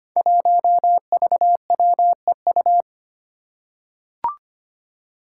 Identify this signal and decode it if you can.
Morse code, heard as a keyed tone, about 25 wpm, 700 Hz. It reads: 1VWEU
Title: Five Challenging Letter-Number Combinations 25wpm